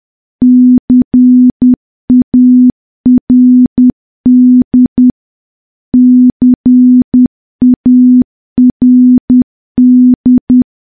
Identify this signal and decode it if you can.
Morse code, heard as a keyed tone, about 10 wpm, 250 Hz. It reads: CARD CARD
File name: morse / morse-training-10WPM